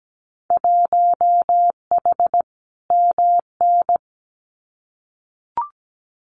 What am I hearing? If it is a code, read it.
Morse code, heard as a keyed tone, about 17 wpm, 700 Hz. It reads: 1H MN
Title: Winter Field Day Exchanges 17wpm